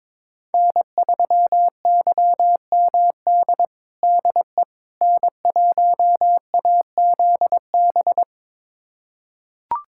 Morse code, heard as a keyed tone, about 21 wpm, 700 Hz. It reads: N3YMD DE N1AZB
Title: USA Callsign DE Callsign - 5 Characters - 22wpm